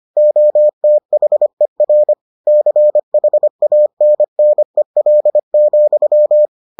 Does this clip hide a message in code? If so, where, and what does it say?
Yes: Morse code from 0.2 to 6.5 s, heard as a keyed tone, about 25 wpm, 600 Hz. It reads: OTHER CHANNEL,